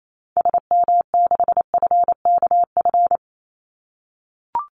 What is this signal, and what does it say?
Morse code, heard as a keyed tone, about 28 wpm, 700 Hz. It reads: SM6FKF
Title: Call Signs - 28wpm